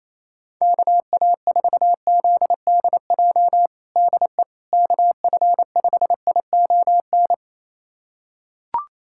Morse code, heard as a keyed tone, about 28 wpm, 700 Hz. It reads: KA4ZDJ DE KF5ION
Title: USA Callsign DE Callsign - 6 Characters - 28wpm